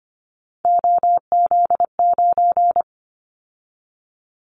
Morse code, heard as a keyed tone, about 25 wpm, 700 Hz. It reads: OZ9
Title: Callsign Prefixes for Top 100 DXCC - 25wpm